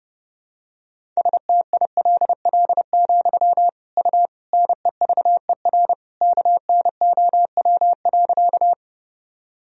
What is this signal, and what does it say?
Morse code, heard as a keyed tone, about 32 wpm, 700 Hz. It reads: STILL, U NEVER KNOW.
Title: Sentences from Top 300 Words and Abbreviations 30wpm